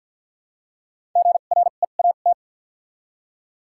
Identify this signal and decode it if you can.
Morse code, heard as a keyed tone, about 53 wpm, spaced as for 37 wpm, 700 Hz. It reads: GREAT
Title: 2x Character Spacing - Top 200 Words - 50wpm